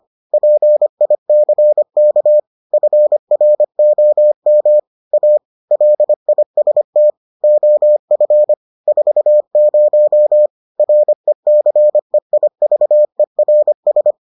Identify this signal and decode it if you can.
Morse code, heard as a keyed tone, about 25 wpm, 600 Hz. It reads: PICK FROM A LIST OF 40 RECEIVERS